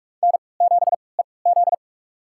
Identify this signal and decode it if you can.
Morse code, heard as a keyed tone, about 46 wpm, spaced as for 25 wpm, 700 Hz. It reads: N7EZ
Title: USA Call Signs - 4 Characters - 3x Character Spacing - 45wpm